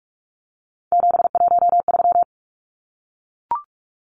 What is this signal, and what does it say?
Morse code, heard as a keyed tone, about 44 wpm, 700 Hz. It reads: 713
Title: Numbers - 3-Digit 45wpm